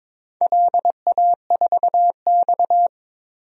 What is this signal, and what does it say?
Morse code, heard as a keyed tone, about 22 wpm, 700 Hz. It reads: LA4X
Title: DXCC-100 Call Signs - 4 Characters - 22wpm